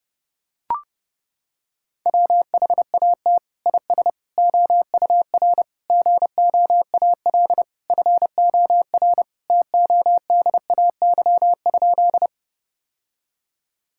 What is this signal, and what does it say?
Morse code, heard as a keyed tone, about 30 wpm, 700 Hz. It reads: WHAT IS OUR GOAL FOR TODAY?
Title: Sentences from Top 1000 Words - Review Part 5 - 30wpm